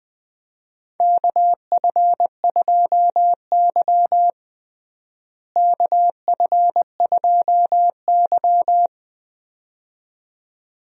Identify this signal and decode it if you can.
Morse code, heard as a keyed tone, about 20 wpm, 700 Hz. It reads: KF2Y KF2Y
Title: USA Call Signs - 4 Characters - Repeated 2x - 20wpm